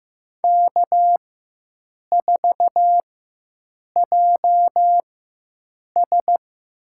Morse code, heard as a keyed tone, about 15 wpm, spaced as for 7 wpm, 700 Hz. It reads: K4JS
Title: USA Call Signs - 4 Characters - 4x Character Spacing - 15wpm